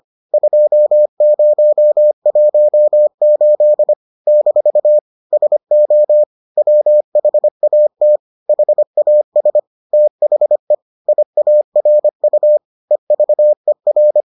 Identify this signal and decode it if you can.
Morse code, heard as a keyed tone, about 25 wpm, 600 Hz. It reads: 2018 - SO WHAT HAS THE IARU EVER